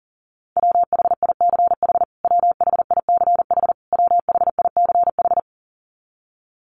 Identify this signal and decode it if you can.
Morse code, heard as a keyed tone, about 40 wpm, 700 Hz. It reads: WHICH WHICH WHICH
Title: Top 300 Words Repeated 40wpm